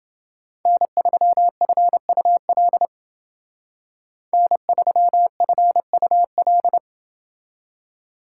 Morse code, heard as a keyed tone, about 29 wpm, 700 Hz. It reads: N3FUL N3FUL